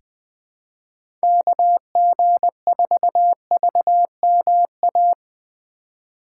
Morse code, heard as a keyed tone, about 20 wpm, 700 Hz. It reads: KG4VMA
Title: USA Call Signs - 6 Characters - 2x Character Spacing - 20wpm